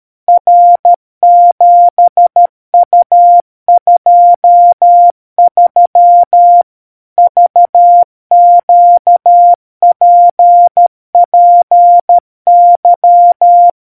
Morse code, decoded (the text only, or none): R7U23 VQPPY